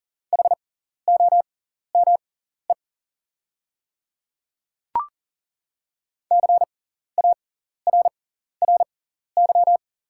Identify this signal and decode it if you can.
Morse code, heard as a keyed tone, about 40 wpm, spaced as for 14 wpm, 690 Hz. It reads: HOME CARRY
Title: 6x Character Spacing - Top 500 Words - 40wpm